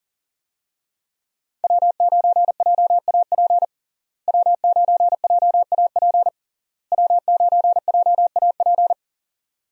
Morse code, heard as a keyed tone, about 40 wpm, 700 Hz. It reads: W9JAP W9JAP W9JAP
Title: USA Call Signs - 5 Characters - Repeated 3x - 40wpm